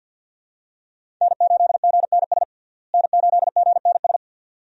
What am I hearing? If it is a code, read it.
Morse code, heard as a keyed tone, about 52 wpm, 700 Hz. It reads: N8GNS N8GNS
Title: USA Call Signs - 5 Characters - Repeated 2x - 50wpm